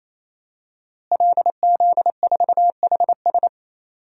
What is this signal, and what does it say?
Morse code, heard as a keyed tone, about 28 wpm, 700 Hz. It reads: LZ4HS
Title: DXCC-100 Call Signs - 5 Characters - Repeated 3x - 28wpm